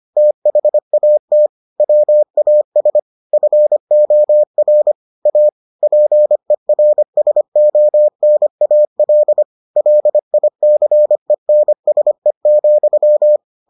Morse code, heard as a keyed tone, about 25 wpm, 600 Hz. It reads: THAT WAS FOR A PERSONAL LICENSE,